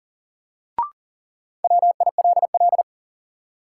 Morse code, heard as a keyed tone, about 40 wpm, 700 Hz. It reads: WILL
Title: Sentences from Top 100 Words - New Word Drill - 40wpm